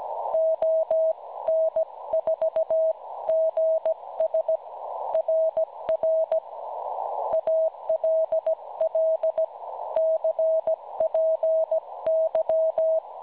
Morse code, heard as a keyed tone, about 17 wpm, 640 Hz. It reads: ON4GS RR ALL CPY
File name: cwronde / websdr recording .6kHz